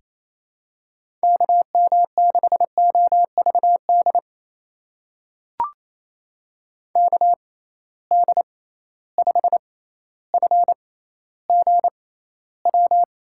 Morse code, heard as a keyed tone, about 28 wpm, spaced as for 14 wpm, 700 Hz. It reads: KM6OVD KD5FGW
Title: USA Call Signs - 6 Characters - 6x Character Spacing - 28wpm